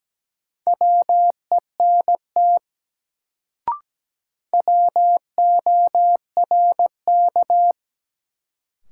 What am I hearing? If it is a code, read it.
Morse code, heard as a keyed tone, about 17 wpm, 700 Hz. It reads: WENT WORK
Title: Sentences from Top 200 Words - New Word Drill - 17wpm